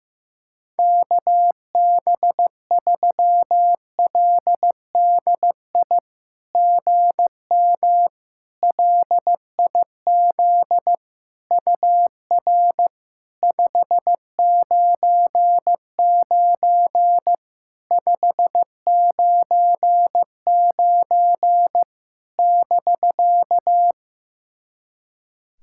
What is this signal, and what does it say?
Morse code, heard as a keyed tone, about 15 wpm, 700 Hz. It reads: KB3LDI GM LIZ UR 599 599 <BK>
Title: SOTA QSO Practice 15wpm